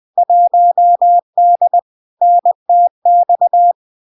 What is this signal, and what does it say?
Morse code, heard as a keyed tone, about 20 wpm, 700 Hz. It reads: 1D NTX
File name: LARK-FD-2022-Log-in-CW-20wpm-700-Hz